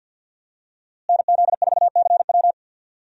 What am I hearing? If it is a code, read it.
Morse code, heard as a keyed tone, about 47 wpm, 700 Hz. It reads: NZ4CW
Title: USA Call Signs - 5 Characters - Repeated 2x - 50wpm